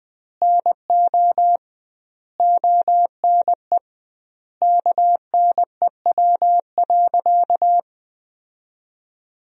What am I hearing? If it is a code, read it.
Morse code, heard as a keyed tone, about 20 wpm, 700 Hz. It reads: NO ONE KNEW.
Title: 2x Word-Spacing - Sentences from Top 400 Words - 20wpm